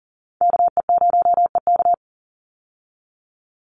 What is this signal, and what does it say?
Morse code, heard as a keyed tone, about 40 wpm, 700 Hz. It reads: KE0EK
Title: Call Signs - 40wpm